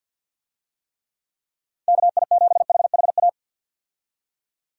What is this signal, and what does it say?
Morse code, heard as a keyed tone, about 53 wpm, 700 Hz. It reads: KI7HHA
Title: USA Call Signs - 6 Characters - 50wpm